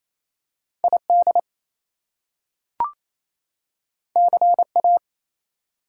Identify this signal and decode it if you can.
Morse code, heard as a keyed tone, about 28 wpm, 700 Hz. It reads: ID CA
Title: US States and Canadian Provinces 28wpm